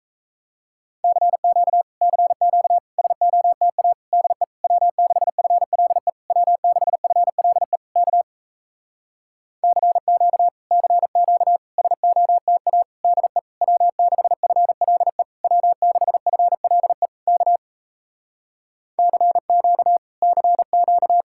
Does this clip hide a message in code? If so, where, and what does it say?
Yes: Morse code from 1.0 to 21.2 s, heard as a keyed tone, about 38 wpm, 700 Hz. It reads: CQ CQ SOTA DE W6FLE W6FLE K CQ CQ SOTA DE W6FLE W6FLE K CQ CQ